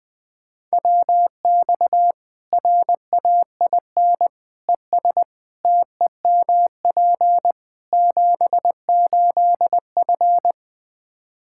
Morse code, heard as a keyed tone, about 20 wpm, 700 Hz. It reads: WX RAIN ES TEMP 78F
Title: K1USN SST QSO Practice 20wpm